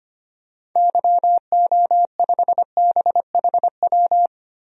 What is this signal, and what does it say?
Morse code, heard as a keyed tone, about 25 wpm, 700 Hz. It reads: YO5BHW
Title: DXCC-100 Call Signs - 6 Characters - Repeated 2x - 25wpm